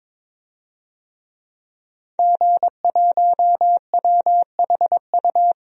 Morse code, heard as a keyed tone, about 22 wpm, 700 Hz. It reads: G1WHU
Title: DXCC-100 Call Signs - 5 Characters - 22wpm